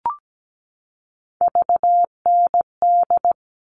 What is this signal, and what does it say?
Morse code, heard as a keyed tone, about 17 wpm, 700 Hz. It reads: VND
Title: All Cut Numbers - 3-Digit 17wpm